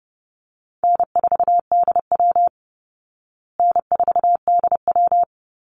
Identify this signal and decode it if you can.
Morse code, heard as a keyed tone, about 28 wpm, 700 Hz. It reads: N4DW N4DW
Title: Easy Call Signs Repeated 2x - 30wpm